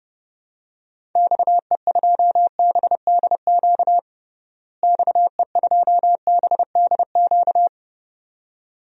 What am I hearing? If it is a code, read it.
Morse code, heard as a keyed tone, about 30 wpm, 700 Hz. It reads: XE2BDQ XE2BDQ
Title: DXCC-100 Call Signs - 6 Characters - Repeated 2x - 30wpm